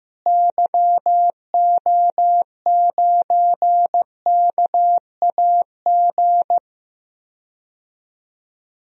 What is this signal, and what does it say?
Morse code, heard as a keyed tone, about 15 wpm, 700 Hz. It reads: YO9KAG
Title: DXCC-100 Call Signs - 6 Characters - Repeated 3x - 15wpm